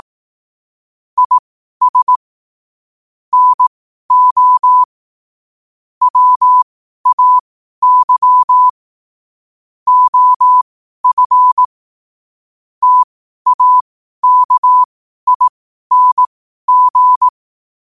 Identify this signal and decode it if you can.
Morse code, heard as a keyed tone, about 18 wpm, spaced as for 12 wpm, 1000 Hz. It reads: IS NO WAY OF TAKING
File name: qotdfiles / 12wpm